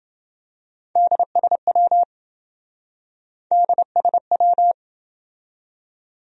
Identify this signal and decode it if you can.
Morse code, heard as a keyed tone, about 27 wpm, 700 Hz. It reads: DSW DSW